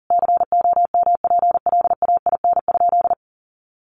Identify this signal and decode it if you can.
Morse code, heard as a keyed tone, about 40 wpm, 700 Hz. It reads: COMPLAIN?
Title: Sentences from Top 1300 Words 40wpm